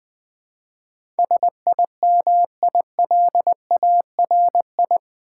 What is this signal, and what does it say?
Morse code, heard as a keyed tone, about 20 wpm, 700 Hz. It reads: SIMILARI
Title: Build A Word - 4 Letters At A Time - 10-Letter Words - 20wpm